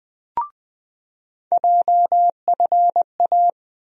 Morse code, heard as a keyed tone, about 21 wpm, 700 Hz. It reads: JFA
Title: USA Call Sign Postfixes - 3 Letters 20wpm